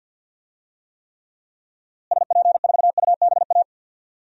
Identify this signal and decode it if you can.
Morse code, heard as a keyed tone, about 49 wpm, 700 Hz. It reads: SP4UBA
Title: DXCC-100 Call Signs - 6 Characters - Repeated 2x - 50wpm